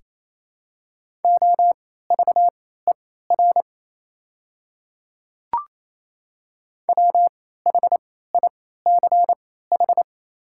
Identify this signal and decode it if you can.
Morse code, heard as a keyed tone, about 28 wpm, spaced as for 16 wpm, 700 Hz. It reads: OVER WHICH
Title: 3x Character Spacing - Top 100 Words - 28wpm